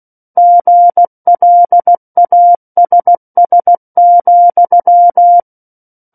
Morse code, heard as a keyed tone, about 16 wpm, 700 Hz. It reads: GLASS,
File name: Book One / 07-Chapter 7 - 16 WPM